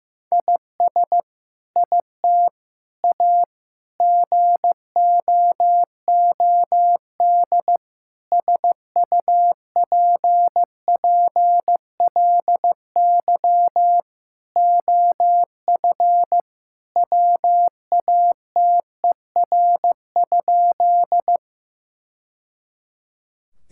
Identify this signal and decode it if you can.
Morse code, heard as a keyed tone, about 15 wpm, 700 Hz. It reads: IS IT A GOOD SUPPLY OF WATER?
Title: Sentences from Top 1000 Words - Review Part 5 - 15wpm